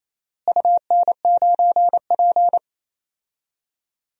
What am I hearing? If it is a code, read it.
Morse code, heard as a keyed tone, about 28 wpm, 700 Hz. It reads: UN9P
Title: DXCC-100 Call Signs - 4 Characters - Repeated 2x - 28wpm